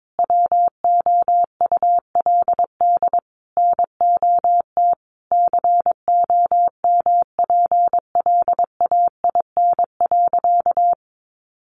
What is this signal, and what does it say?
Morse code, heard as a keyed tone, about 22 wpm, 700 Hz. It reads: WOULD NOT COMPLAIN.
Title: Sentences from Top 1300 Words 22wpm